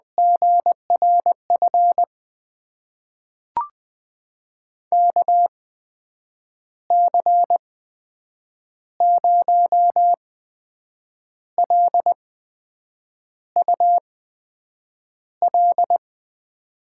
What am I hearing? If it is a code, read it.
Morse code, heard as a keyed tone, about 21 wpm, spaced as for 7 wpm, 700 Hz. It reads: GRF KC0LUL
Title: USA Call Signs - 6 Characters - 8x Character Spacing - 20wpm